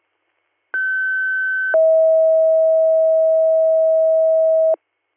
Two-tone page, heard as a one-second tone, then a three-second tone, about 1550 then 640 Hz. It is heard from 0.7 to 4.7 s.